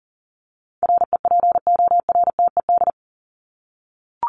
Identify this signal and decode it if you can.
Morse code, heard as a keyed tone, about 40 wpm, 700 Hz. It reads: REPORTED
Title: ED Suffix - Top 100 Words with Base Words 40wpm